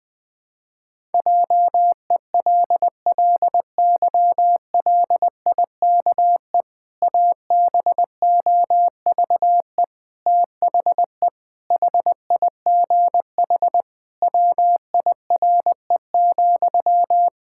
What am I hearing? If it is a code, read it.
Morse code, heard as a keyed tone, about 20 wpm, 700 Hz. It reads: JELLYLIKE ABOVE THE HIGH WIRE,
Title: Pangram Challenge v2 - 20wpm